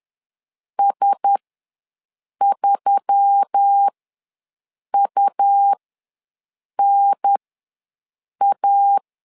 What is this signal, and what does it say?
Morse code, heard as a keyed tone, about 10 wpm, spaced as for 6 wpm, 800 Hz. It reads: S3UNA
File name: LEZIONI / 4 Q1S3 ws 10wpm 2437